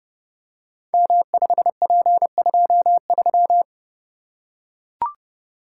Morse code, heard as a keyed tone, about 30 wpm, 700 Hz. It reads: M5P23
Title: Code Groups - 5 Letter:Numbers - 30wpm